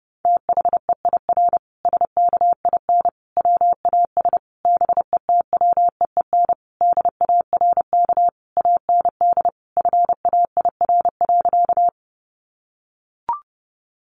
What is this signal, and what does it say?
Morse code, heard as a keyed tone, about 30 wpm, 700 Hz. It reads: THEIR SKIN WAS BETWEEN DARK AND FAIR.
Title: Sentences from Top 1300 Words 30wpm